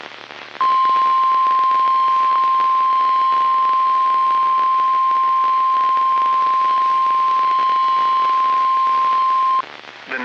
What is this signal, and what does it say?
Emergency Alert System (SAME) tones (the 1050 Hz attention tone). It is heard from 0.5 to 9.7 s.